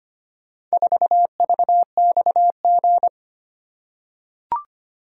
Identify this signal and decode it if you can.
Morse code, heard as a keyed tone, about 25 wpm, 700 Hz. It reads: 4VXG
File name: practice / 25wpm / Four Letter-Number Combinations - 25wpm